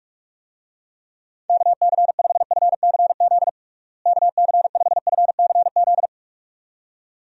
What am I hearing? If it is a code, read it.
Morse code, heard as a keyed tone, about 45 wpm, 700 Hz. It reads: KC5FCZ KC5FCZ
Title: USA Call Signs - 6 Characters - Repeated 2x - 45wpm